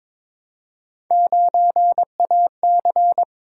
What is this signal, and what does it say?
Morse code, heard as a keyed tone, about 22 wpm, 700 Hz. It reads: 9AC
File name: practice / 22wpm / Three Letter-Number Combinations - 22wpm